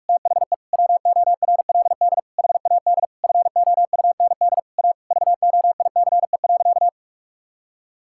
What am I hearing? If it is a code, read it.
Morse code, heard as a keyed tone, about 46 wpm, 700 Hz. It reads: THE WORLD HAD FOUND A VOICE.
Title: Sentences from Top 1600 Words 45wpm